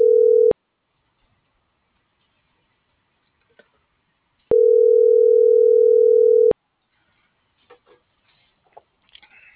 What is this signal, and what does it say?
Ringback tone (North America), 440 + 480 Hz.